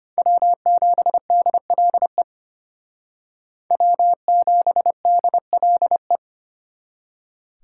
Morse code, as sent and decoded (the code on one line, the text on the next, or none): .-- --... -.. .-.. / .-- --... -.. .-.. .
W7DLE W7DLE